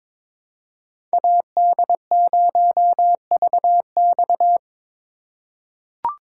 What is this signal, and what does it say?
Morse code, heard as a keyed tone, about 21 wpm, 700 Hz. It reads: AD0VX
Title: USA Call Signs - 2x Character Spacing - 22wpm